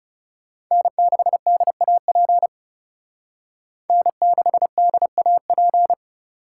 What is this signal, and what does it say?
Morse code, heard as a keyed tone, about 38 wpm, 700 Hz. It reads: N6DAP N6DAP